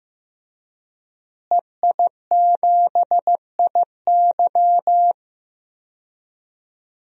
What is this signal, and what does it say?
Morse code, heard as a keyed tone, about 15 wpm, 700 Hz. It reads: EI7IY
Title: DXCC-100 Call Signs - 5 Characters - Repeated 3x - 15wpm